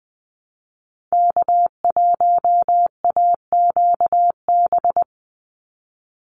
Morse code, heard as a keyed tone, about 20 wpm, 700 Hz. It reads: K1AQB
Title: Call Signs Repeated 3x - 20wpm